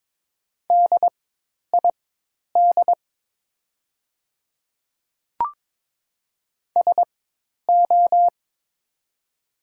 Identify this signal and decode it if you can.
Morse code, heard as a keyed tone, about 22 wpm, spaced as for 10 wpm, 700 Hz. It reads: DID SO
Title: 4x Character Spacing - Top 100 Words - 22wpm